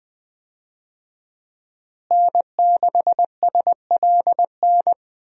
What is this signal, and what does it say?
Morse code, heard as a keyed tone, about 20 wpm, 700 Hz. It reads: N6SLN
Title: USA Call Signs - 5 Characters - Repeated 2x - 20wpm